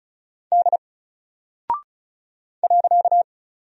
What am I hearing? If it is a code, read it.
Morse code, heard as a keyed tone, about 36 wpm, 700 Hz. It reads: D .
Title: Single Character (Letter:Number:Punctuation:Prosign) 35wpm